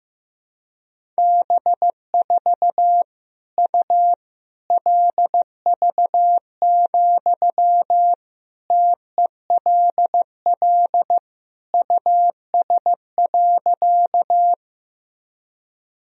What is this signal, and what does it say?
Morse code, heard as a keyed tone, about 15 wpm, 700 Hz. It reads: B4 U LV, TELL US.
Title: Sentences from Top 400 Words and Abbreviations 15wpm